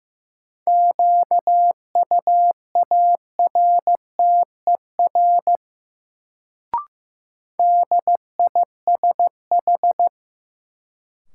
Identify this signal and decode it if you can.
Morse code, heard as a keyed tone, about 15 wpm, 700 Hz. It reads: QUARTER DISH
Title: Sentences from Top 1300 Words - New Word Drill - 15wpm